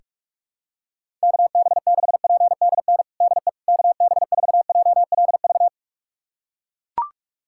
Morse code, heard as a keyed tone, about 41 wpm, 700 Hz. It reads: KB6PDN DE KB4JLV
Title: USA Callsign DE Callsign - 6 Characters - 45wpm